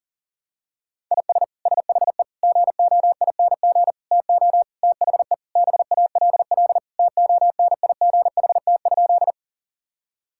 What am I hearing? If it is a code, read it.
Morse code, heard as a keyed tone, about 40 wpm, 700 Hz. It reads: IS SHE GOING TO THE BALL TONIGHT?
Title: Sentences from Top 1000 Words - Review Part 6 - 40wpm